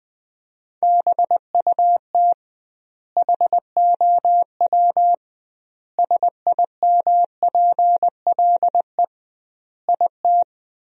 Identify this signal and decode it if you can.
Morse code, heard as a keyed tone, about 20 wpm, 700 Hz. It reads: BUT HOW SIMPLE IT
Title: 2x Word-Spacing - Sentences from Top 400 Words - 20wpm